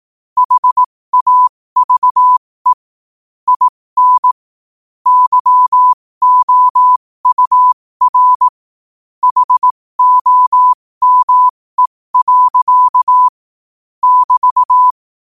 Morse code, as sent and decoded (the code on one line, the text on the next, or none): .... .- ...- . / .. -. / -.-- --- ..- .-. / .... --- -- . .-.-.- / -...-
HAVE IN YOUR HOME. <BT>